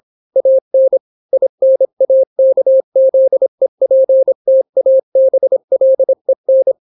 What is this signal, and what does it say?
Morse code, heard as a keyed tone, about 26 wpm, 530 Hz. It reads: AN INAKZEPTABLEN